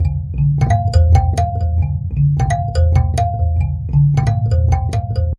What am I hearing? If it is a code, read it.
There is no code.